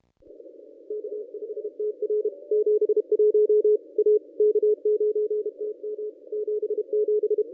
Morse code, heard as a keyed tone, about 32 wpm, 440 Hz. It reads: K5TR 71 A K9TM 77
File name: contests / ss-cw-k5tr-op-n5rz -1100